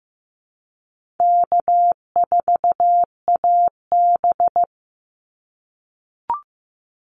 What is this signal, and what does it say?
Morse code, heard as a keyed tone, about 15 wpm, 700 Hz. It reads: K4AB
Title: Easy Call Signs - 15wpm